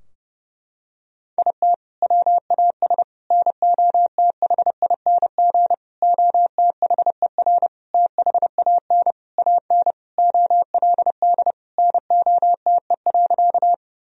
Morse code, heard as a keyed tone, about 30 wpm, 700 Hz. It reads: IT WAS NOTHING OTHER THAN AN OLD NOTE.
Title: Sentences from Top 1000 Words - Review Part 2 - 30wpm